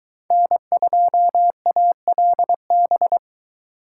Morse code, heard as a keyed tone, about 22 wpm, 700 Hz. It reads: N2ALB